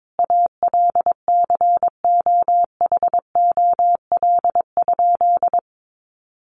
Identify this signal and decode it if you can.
Morse code, heard as a keyed tone, about 22 wpm, 700 Hz. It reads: ALCOHOL?
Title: Sentences from Top 1400 Words 22wpm